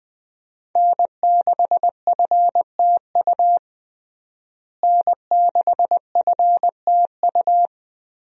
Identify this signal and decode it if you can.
Morse code, heard as a keyed tone, about 20 wpm, 700 Hz. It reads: N6FTU N6FTU
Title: USA Call Signs - 5 Characters - Repeated 3x - 20wpm